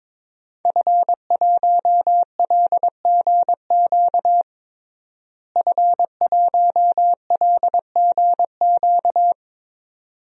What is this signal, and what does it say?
Morse code, heard as a keyed tone, about 21 wpm, 700 Hz. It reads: F1LGQ F1LGQ
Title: DXCC-100 Call Signs - 5 Characters - Repeated 2x - 22wpm